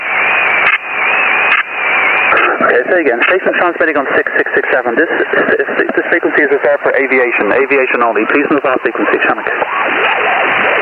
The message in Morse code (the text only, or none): none